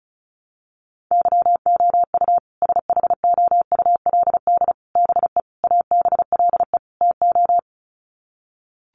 Morse code, heard as a keyed tone, about 34 wpm, 700 Hz. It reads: YOU SHOULD BE ABLE TO
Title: Sets of 5 Words Encore 35wpm